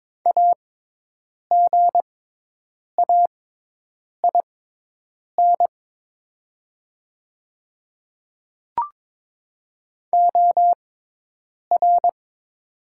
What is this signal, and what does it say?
Morse code, heard as a keyed tone, about 22 wpm, spaced as for 8 wpm, 700 Hz. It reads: AGAIN OR
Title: 6x Character Spacing - Top 300 Words - 22wpm